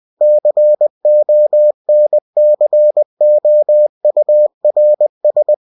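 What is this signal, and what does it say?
Morse code, heard as a keyed tone, about 20 wpm, 600 Hz. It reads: CONCOURS
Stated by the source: UFT bulletin sent in CW.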